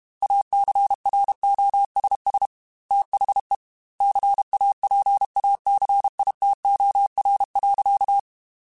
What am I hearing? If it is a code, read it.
Morse code, heard as a keyed tone, about 32 wpm, 800 Hz. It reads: ACROSS THE CAPACITOR.
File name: Technical Subjects / Technical Subjects-mp3s / Reactance @30